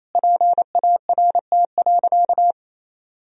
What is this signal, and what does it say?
Morse code, heard as a keyed tone, about 28 wpm, 700 Hz. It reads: PART.
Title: Sentences from Top 1000 Words - Review Part 3 - 28wpm